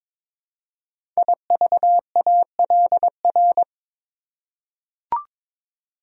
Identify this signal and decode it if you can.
Morse code, heard as a keyed tone, about 22 wpm, 700 Hz. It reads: IVALR
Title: Code Groups - 5 Letters - 22wpm